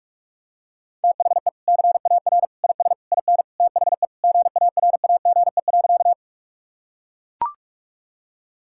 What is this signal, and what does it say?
Morse code, heard as a keyed tone, about 41 wpm, 700 Hz. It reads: THE CAR IS IN THE GARAGE.
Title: Sentences from Top 1600 Words 45wpm